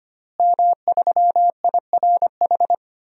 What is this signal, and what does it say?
Morse code, heard as a keyed tone, about 25 wpm, 700 Hz. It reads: M3IRH
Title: DXCC-100 Call Signs - 5 Characters - Repeated 2x - 25wpm